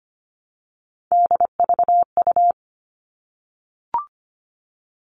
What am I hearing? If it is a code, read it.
Morse code, heard as a keyed tone, about 25 wpm, 700 Hz. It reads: DVU
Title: All Cut Numbers - 3-Digit 25wpm